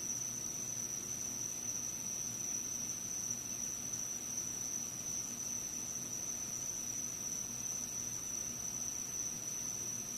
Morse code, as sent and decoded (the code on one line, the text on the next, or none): none